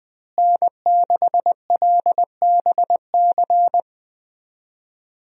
Morse code, heard as a keyed tone, about 20 wpm, 700 Hz. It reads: N6LBC
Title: USA Call Signs - 5 Characters - Repeated 3x - 20wpm